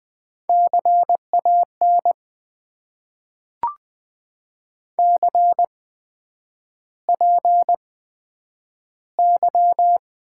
Morse code, heard as a keyed tone, about 20 wpm, spaced as for 9 wpm, 700 Hz. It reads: CAN CPY
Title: Top 90 QSO Elements - 8x Character Spacing - 20wpm